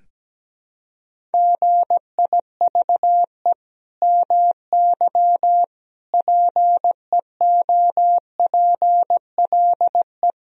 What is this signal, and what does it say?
Morse code, heard as a keyed tone, about 17 wpm, 700 Hz. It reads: GIVE MY PEOPLE
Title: Sentences from Top 1000 Words - Review Part 1 - 17wpm